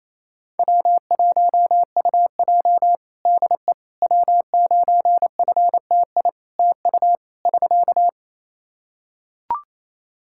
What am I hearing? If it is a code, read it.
Morse code, heard as a keyed tone, about 27 wpm, 700 Hz. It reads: W1UJ DE W9FTI TU <SK>
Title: Practice QSO - Exchange 4 - Ending - 28wpm